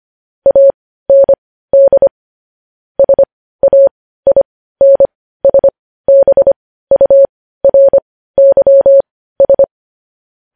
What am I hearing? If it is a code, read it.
Morse code, heard as a keyed tone, about 25 wpm, spaced as for 15 wpm, 550 Hz. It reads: AND SAINSBURYS
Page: news-headlines-in-morse-code-at-15-wpm